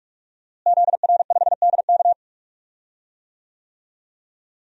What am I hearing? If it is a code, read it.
Morse code, heard as a keyed tone, about 47 wpm, 700 Hz. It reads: ZR5DK
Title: DXCC-100 Call Signs - 5 Characters - Repeated 2x - 45wpm